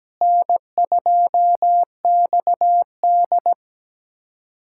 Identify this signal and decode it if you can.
Morse code, heard as a keyed tone, about 17 wpm, 700 Hz. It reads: N2XD
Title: USA Callsign DE Callsign - 4 Characters - 17wpm